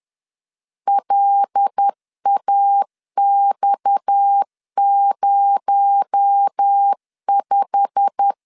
Morse code, heard as a keyed tone, about 10 wpm, 800 Hz. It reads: LAX05
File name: LEZIONI / 9 2867 10wpm 2733